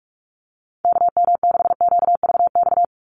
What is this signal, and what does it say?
Morse code, heard as a keyed tone, about 44 wpm, 700 Hz. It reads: KM6QVX
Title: USA Call Signs 45wpm